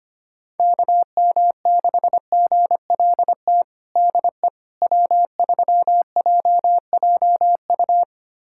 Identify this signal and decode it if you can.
Morse code, heard as a keyed tone, about 25 wpm, 700 Hz. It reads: KM6GLT DE W3JJU
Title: USA Callsign DE Callsign 25wpm